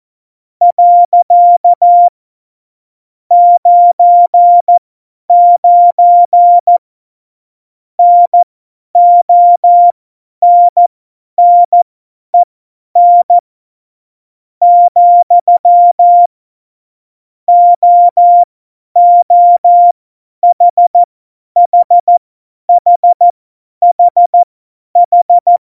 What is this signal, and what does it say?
Morse code, heard as a keyed tone, about 14 wpm, spaced as for 9 wpm, 700 Hz. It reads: . 99 NONNEN, OOHHHHH
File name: Morsekurs / Witze / Witz 12. Januar 2021, 50 F70